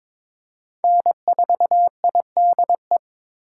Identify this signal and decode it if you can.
Morse code, heard as a keyed tone, about 21 wpm, 700 Hz. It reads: N4IDE
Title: USA Call Signs - 8x Character Spacing - 22wpm